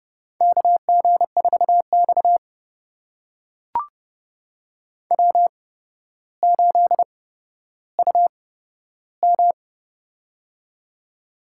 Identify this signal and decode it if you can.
Morse code, heard as a keyed tone, about 30 wpm, spaced as for 13 wpm, 700 Hz. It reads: KG4X W8UM
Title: USA Call Signs - 4 Characters - 8x Character Spacing - 30wpm